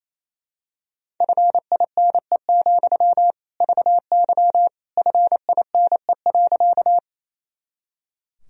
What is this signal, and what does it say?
Morse code, heard as a keyed tone, about 28 wpm, 700 Hz. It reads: FINE, VY FINE.
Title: Sentences from Top 500 Words and Abbreviations 28wpm